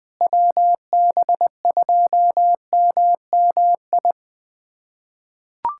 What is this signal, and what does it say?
Morse code, heard as a keyed tone, about 20 wpm, 700 Hz. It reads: WB2MMI
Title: USA Callsign DE Callsign - 6 Characters - 20wpm